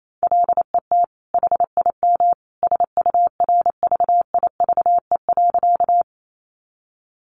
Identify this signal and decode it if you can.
Morse code, heard as a keyed tone, about 28 wpm, 700 Hz. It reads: LET HIM SURVIVE.
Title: Sentences from Top 1300 Words 28wpm